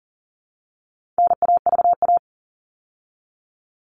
Morse code, heard as a keyed tone, about 40 wpm, 700 Hz. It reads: NAVA
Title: All Cut Numbers - 4-Digit 40wpm